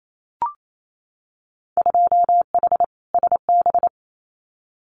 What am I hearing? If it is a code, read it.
Morse code, heard as a keyed tone, about 28 wpm, 700 Hz. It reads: 2H SB
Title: Winter Field Day Exchanges Encore 28wpm